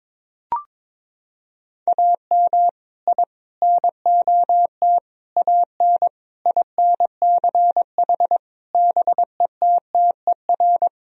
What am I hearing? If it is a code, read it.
Morse code, heard as a keyed tone, about 22 wpm, 700 Hz. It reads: AM I NOT AN INCH BETTER
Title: Sentences from Top 1000 Words - Review Part 3 - 22wpm